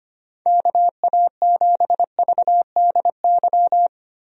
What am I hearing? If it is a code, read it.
Morse code, heard as a keyed tone, about 25 wpm, 700 Hz. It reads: KA7VDY
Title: USA Call Signs - 6 Characters - Repeated 2x - 25wpm